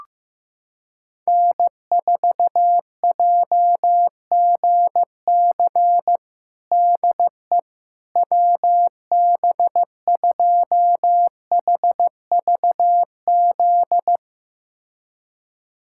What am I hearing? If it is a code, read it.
Morse code, heard as a keyed tone, about 15 wpm, 700 Hz. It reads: N4JGC DE WB2HVZ
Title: USA Callsign DE Callsign 15wpm